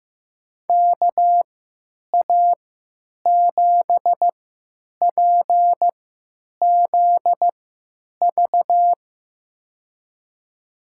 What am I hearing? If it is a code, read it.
Morse code, heard as a keyed tone, about 15 wpm, spaced as for 9 wpm, 700 Hz. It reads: KA7PZV
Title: USA Call Signs - 6 Characters - 3x Character Spacing - 15wpm